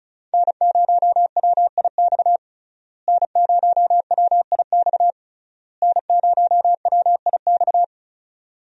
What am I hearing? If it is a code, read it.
Morse code, heard as a keyed tone, about 34 wpm, 700 Hz. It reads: N0WIX N0WIX N0WIX
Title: USA Call Signs - 5 Characters - Repeated 3x - 35wpm